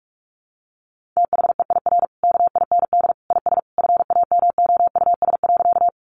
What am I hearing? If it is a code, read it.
Morse code, heard as a keyed tone, about 44 wpm, 700 Hz. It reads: THEIR KIND IS FAMOUS.
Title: Sentences from Top 1500 Words 45wpm